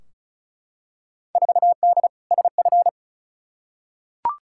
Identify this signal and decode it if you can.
Morse code, heard as a keyed tone, about 36 wpm, 700 Hz. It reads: 4D SF
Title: Field Day Exchanges Encore - 35wpm